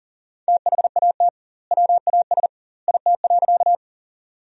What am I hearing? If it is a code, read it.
Morse code, heard as a keyed tone, about 40 wpm, 700 Hz. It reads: THAT WAS IT.
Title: 2x Word-Spacing - Sentences from Top 100 Words - 40wpm